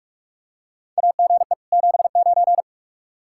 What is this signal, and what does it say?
Morse code, heard as a keyed tone, about 46 wpm, 700 Hz. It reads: AGE 79
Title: K1USN SST QSO Practice 45wpm